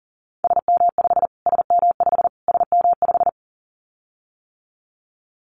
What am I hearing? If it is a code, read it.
Morse code, heard as a keyed tone, about 40 wpm, 700 Hz. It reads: SM5 SM5 SM5
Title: Callsign Prefixes for Top 25 DXCC - Repeated 3x - 40wpm